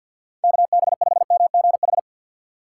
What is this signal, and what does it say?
Morse code, heard as a keyed tone, about 53 wpm, 700 Hz. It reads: KB5MGH
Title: USA Call Signs - 6 Characters - 6x Character Spacing - 50wpm